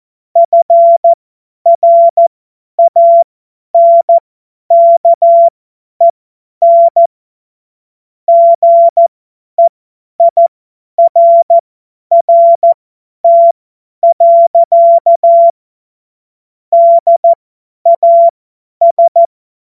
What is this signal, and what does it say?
Morse code, heard as a keyed tone, about 14 wpm, spaced as for 10 wpm, 680 Hz. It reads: FRANKEN GEIRRT. DAS
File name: Morsekurs / Witze / Witz 20. Januar 2026 50 F70